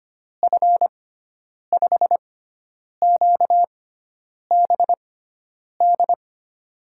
Morse code, heard as a keyed tone, about 25 wpm, spaced as for 9 wpm, 700 Hz. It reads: F5QBD
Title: USA Call Signs - 6 Characters - 6x Character Spacing - 25wpm